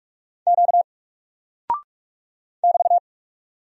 Morse code, heard as a keyed tone, about 46 wpm, 700 Hz. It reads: Q <BT>
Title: Single Character (Letter:Number:Punctuation:Prosign) 45wpm